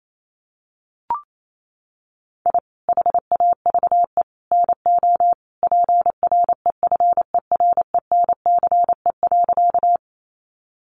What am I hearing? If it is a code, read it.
Morse code, heard as a keyed tone, about 28 wpm, 700 Hz. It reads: I HAVE NO PREFERENCE.
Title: Sentences from Top 1500 Words 28wpm